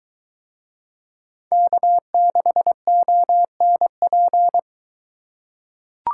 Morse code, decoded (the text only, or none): K6ONP